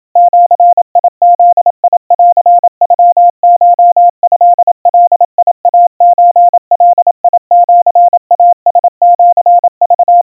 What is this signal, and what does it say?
Morse code (Japanese Wabun code), heard as a keyed tone, about 27 wpm, 700 Hz. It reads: ジブンノコトガイソガシイラシク
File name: kawa / Morse sample / kamyu 100 cpm 700 Hz Crispy